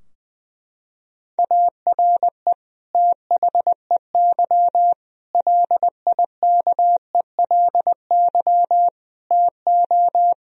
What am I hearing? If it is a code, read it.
Morse code, heard as a keyed tone, about 20 wpm, 700 Hz. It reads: ARE THEY LIKELY TO
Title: Sentences from Top 1000 Words - Review Part 4 - 20wpm